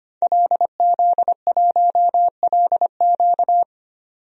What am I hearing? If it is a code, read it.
Morse code, heard as a keyed tone, about 25 wpm, 690 Hz. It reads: LZ1LQ
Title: DXCC-100 Call Signs - 5 Characters - 25wpm